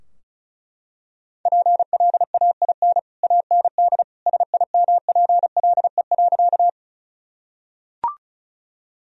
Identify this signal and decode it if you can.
Morse code, heard as a keyed tone, about 34 wpm, 700 Hz. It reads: PLAIN AND SIMPLE.
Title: Sentences from Top 1000 Words - Review Part 2 - 35wpm